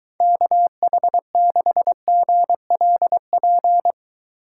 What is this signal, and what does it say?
Morse code, heard as a keyed tone, about 23 wpm, 700 Hz. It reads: KH6GLP